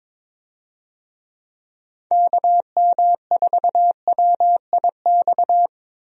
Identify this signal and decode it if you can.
Morse code, heard as a keyed tone, about 21 wpm, 700 Hz. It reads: KM4WIX
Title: USA Call Signs - 6 Characters - 22wpm